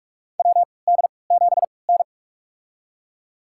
Morse code, heard as a keyed tone, about 41 wpm, spaced as for 25 wpm, 700 Hz. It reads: WD7N
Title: USA Call Signs - 3x Character Spacing - 45wpm